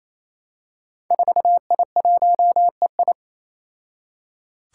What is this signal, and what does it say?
Morse code, heard as a keyed tone, about 28 wpm, 700 Hz. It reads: 4I1EI
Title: Five Challenging Letter-Number Combinations 28wpm